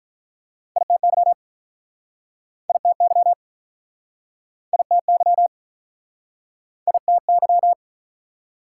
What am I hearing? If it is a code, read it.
Morse code, heard as a keyed tone, about 47 wpm, 690 Hz. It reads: ITY ITY ITY ITY